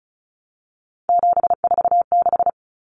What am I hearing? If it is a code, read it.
Morse code, heard as a keyed tone, about 34 wpm, 700 Hz. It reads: 746
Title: Common Cut Numbers - 3-Digit 35wpm